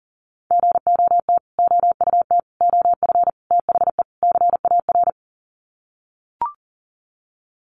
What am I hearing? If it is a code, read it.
Morse code, heard as a keyed tone, about 40 wpm, 700 Hz. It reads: GOT OUT OF THE CAR
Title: Sets of 5 Words Encore 40wpm